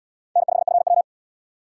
Morse code, heard as a keyed tone, about 68 wpm, 700 Hz. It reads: N5FW